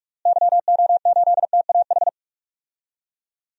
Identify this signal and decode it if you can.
Morse code, heard as a keyed tone, about 43 wpm, 700 Hz. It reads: YO8TAH
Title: DXCC-100 Call Signs - 6 Characters - Repeated 2x - 45wpm